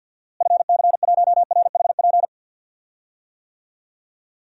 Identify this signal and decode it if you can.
Morse code, heard as a keyed tone, about 52 wpm, 700 Hz. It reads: FC1RHP
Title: DXCC-100 Call Signs - 6 Characters - Repeated 3x - 50wpm